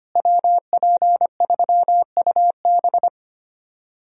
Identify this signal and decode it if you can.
Morse code, heard as a keyed tone, about 25 wpm, 700 Hz. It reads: WP3UB
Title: USA Call Signs - 5 Characters - Repeated 2x - 25wpm